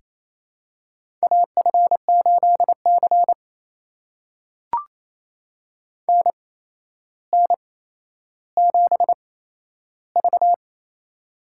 Morse code, heard as a keyed tone, about 28 wpm, spaced as for 12 wpm, 700 Hz. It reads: AF8C NN7V
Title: USA Call Signs - 4 Characters - 8x Character Spacing - 28wpm